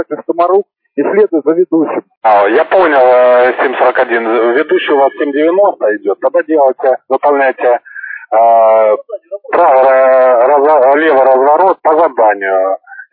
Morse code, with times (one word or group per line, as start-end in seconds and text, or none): none